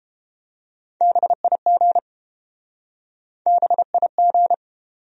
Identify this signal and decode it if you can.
Morse code, heard as a keyed tone, about 31 wpm, 700 Hz. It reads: BIG BIG